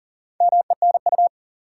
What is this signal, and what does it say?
Morse code, heard as a keyed tone, about 40 wpm, 700 Hz. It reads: MENU
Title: Four Letter Words - 40wpm